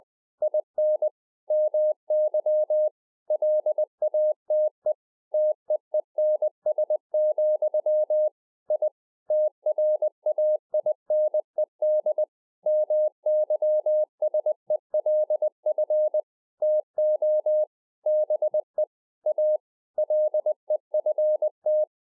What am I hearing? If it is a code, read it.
Morse code, heard as a keyed tone, about 20 wpm, 610 Hz. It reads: IN MY LATE TEENS, I TRAINED MYSELF TO BE A LEFT